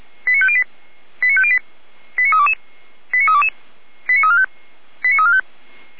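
Five-tone selective call (ZVEI), heard as quick runs of five tones, tones of about 70 ms. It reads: 89508 89508 89322 89322 89345 89345
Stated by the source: ZVEI five-tone selective call received via SDR, from the Feuerwehr Orlamünde fire brigade audio server.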